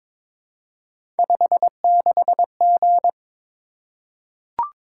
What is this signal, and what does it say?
Morse code, heard as a keyed tone, about 22 wpm, 700 Hz. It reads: 56G
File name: practice / 22wpm / Three Letter-Number Combinations - 22wpm